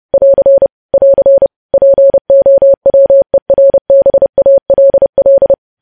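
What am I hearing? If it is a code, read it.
Morse code, heard as a keyed tone, about 30 wpm, 550 Hz. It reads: <AR> <AR> POWERBALL
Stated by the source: Morse code news headlines sent at 30 words per minute.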